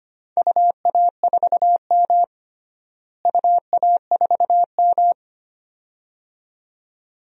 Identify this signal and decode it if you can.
Morse code, heard as a keyed tone, about 25 wpm, 700 Hz. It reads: UA4M UA4M
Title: DXCC-100 Call Signs - 4 Characters - Repeated 2x - 25wpm